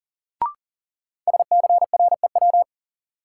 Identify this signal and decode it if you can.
Morse code, heard as a keyed tone, about 40 wpm, 700 Hz. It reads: SCREW
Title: Sentences from Top 1600 Words - New Word Drill - 40wpm